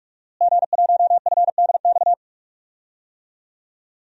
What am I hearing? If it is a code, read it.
Morse code, heard as a keyed tone, about 43 wpm, 700 Hz. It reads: G1FDX
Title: DXCC-100 Call Signs - 5 Characters - Repeated 3x - 45wpm